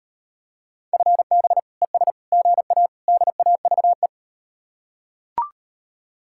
Morse code, heard as a keyed tone, about 38 wpm, 700 Hz. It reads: FB ES GA DAVE